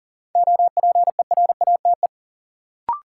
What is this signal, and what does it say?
Morse code, heard as a keyed tone, about 40 wpm, 700 Hz. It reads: OPERATE
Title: Sentences from Top 1500 Words - New Word Drill - 40wpm